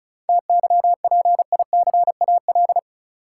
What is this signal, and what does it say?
Morse code, heard as a keyed tone, about 34 wpm, 700 Hz. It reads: TYPICAL
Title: Sentences from Top 1100 Words - New Word Drill Alt - 35wpm